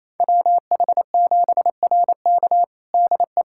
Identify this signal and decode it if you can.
Morse code, heard as a keyed tone, about 27 wpm, 700 Hz. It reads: WH7RK DE
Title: USA Callsign DE Callsign - 5 Characters - 28wpm